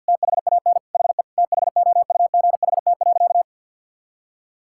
Morse code, heard as a keyed tone, about 50 wpm, 700 Hz. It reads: THAN HE THOUGHT.
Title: Sentences from Top 1600 Words 50wpm